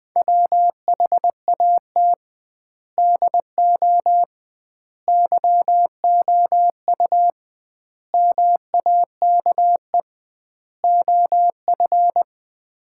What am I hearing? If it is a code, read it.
Morse code, heard as a keyed tone, about 20 wpm, 700 Hz. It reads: WHAT DO YOU MAKE OF
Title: 2x Word-Spacing - Sets of 5 Words - 20wpm